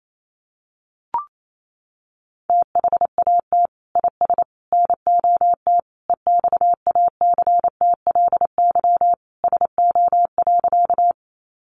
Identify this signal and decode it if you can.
Morse code, heard as a keyed tone, about 28 wpm, 700 Hz. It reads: THAT IS NOT EXACTLY SO.
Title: Sentences from Top 900 Words 28wpm